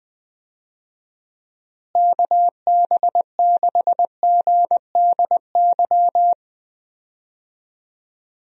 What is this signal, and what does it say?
Morse code, heard as a keyed tone, about 21 wpm, 700 Hz. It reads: KB6GDY
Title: USA Call Signs - 6 Characters - Repeated 3x - 20wpm